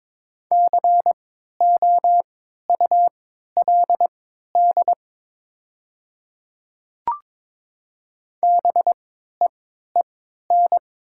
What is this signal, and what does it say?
Morse code, heard as a keyed tone, about 22 wpm, spaced as for 12 wpm, 700 Hz. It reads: COULD BEEN
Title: 3x Character Spacing - Top 100 Words - 22wpm